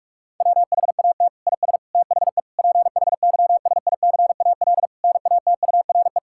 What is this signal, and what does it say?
Morse code, heard as a keyed tone, about 46 wpm, 700 Hz. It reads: WHAT IS THE PHYSICAL NATURE
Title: Sentences from Top 1000 Words - Review Part 6 - 45wpm